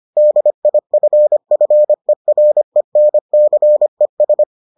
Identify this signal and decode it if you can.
Morse code, heard as a keyed tone, about 25 wpm, 600 Hz. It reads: DIFFERENCES